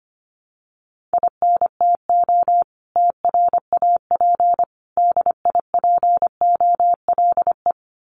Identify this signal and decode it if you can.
Morse code, heard as a keyed tone, about 25 wpm, 700 Hz. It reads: INTO TRAP DIPOLE
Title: Common QSO Phrases 25wpm